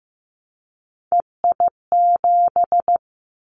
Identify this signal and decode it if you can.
Morse code, heard as a keyed tone, about 15 wpm, 700 Hz. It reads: EI7
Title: Callsign Prefixes for Top 50 DXCC - 15wpm